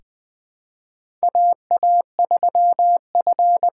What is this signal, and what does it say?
Morse code, heard as a keyed tone, about 20 wpm, 700 Hz. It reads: AA3F
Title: USA Call Signs - 5 Characters - 6x Character Spacing - 20wpm